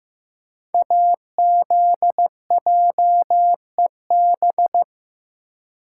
Morse code, heard as a keyed tone, about 15 wpm, 700 Hz. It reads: AZJEB
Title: Code Groups - 5 Letter:Numbers - 15wpm